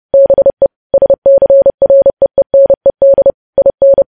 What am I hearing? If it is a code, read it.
Morse code, heard as a keyed tone, about 30 wpm, 550 Hz. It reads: BE SCREENED IN
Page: news headlines in morse code at 30 wpm